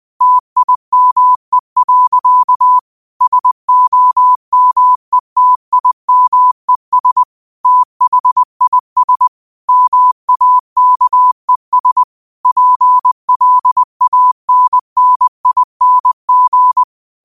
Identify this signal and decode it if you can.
Morse code, heard as a keyed tone, about 20 wpm, 1000 Hz. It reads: TIME. SOMETIMES THIS MAKES PLANNING
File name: qotdfiles / 20wpm